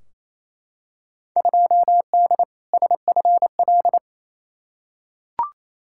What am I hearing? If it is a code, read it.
Morse code, heard as a keyed tone, about 28 wpm, 700 Hz. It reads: 2D SFL
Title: Field Day Exchanges Encore - 28wpm